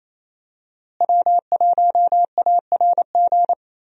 Morse code, heard as a keyed tone, about 28 wpm, 700 Hz. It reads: W1ARG
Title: USA Call Signs - 5 Characters - Repeated 2x - 28wpm